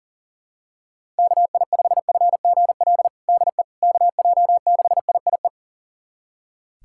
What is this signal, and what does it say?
Morse code, heard as a keyed tone, about 40 wpm, 700 Hz. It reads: KI5FGL DE KJ6IIE
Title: USA Callsign DE Callsign - 6 Characters - 40wpm